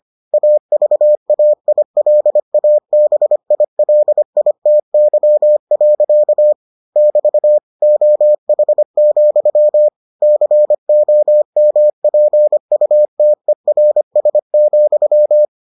Morse code, heard as a keyed tone, about 25 wpm, 600 Hz. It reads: AVAILABILITY. <BT> OH, COMPUTERS,